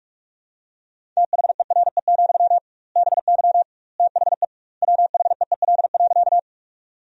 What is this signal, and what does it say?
Morse code, heard as a keyed tone, about 46 wpm, 700 Hz. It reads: THERE, BY THE WHEEL.
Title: 2x Word-Spacing - Sentences from Top 500 Words - 45wpm